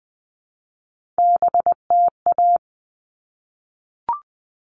Morse code, heard as a keyed tone, about 20 wpm, 700 Hz. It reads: BTA
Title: All Cut Numbers - 3-Digit 20wpm